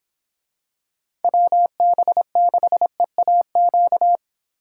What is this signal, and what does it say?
Morse code, heard as a keyed tone, about 25 wpm, 700 Hz. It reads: WB6EAQ
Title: Speed Racing - USA 2x3 Call Signs 22wpm